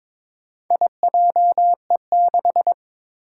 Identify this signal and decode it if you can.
Morse code, heard as a keyed tone, about 22 wpm, 700 Hz. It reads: IJE6
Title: Four Challenging Letter-Number Combinations 22wpm